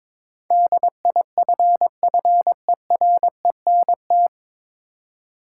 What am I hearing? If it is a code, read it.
Morse code, heard as a keyed tone, about 22 wpm, 700 Hz. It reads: DIFFERENT
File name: 9 Letter Words - First 3 Letters Given - 22wpm